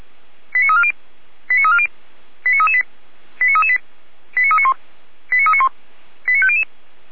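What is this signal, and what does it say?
Five-tone selective call (ZVEI), heard as quick runs of five tones, tones of about 80 ms. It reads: 89340 89340 89338 89338 89391 89391 89500
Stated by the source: ZVEI five-tone selective call 89315 received via SDR, Feuerwehr Orlamünde fire brigade.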